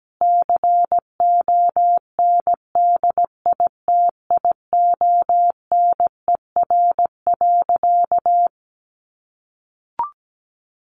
Morse code, heard as a keyed tone, about 17 wpm, 700 Hz. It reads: CONDITIONER.
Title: Additional Easy Phrases 17wpm